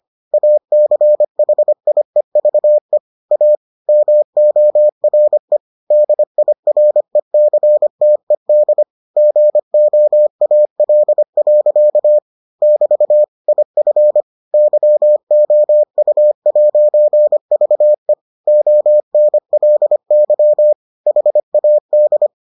Morse code, heard as a keyed tone, about 25 wpm, 600 Hz. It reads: ACHIEVE A MORE DIRECTED GOAL. <BT> IF YOU'VE ONLY HAD